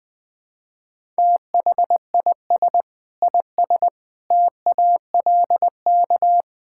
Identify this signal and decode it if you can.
Morse code, heard as a keyed tone, about 20 wpm, 700 Hz. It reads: THIS IS TALK
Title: Build A Phrase - 5-Word Phrases - 20wpm